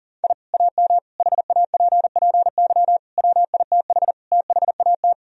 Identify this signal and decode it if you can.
Morse code, heard as a keyed tone, about 40 wpm, 700 Hz. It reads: I AM HAPPY WITH THAT
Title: Sentences from Top 1000 Words - Review Part 2 - 40wpm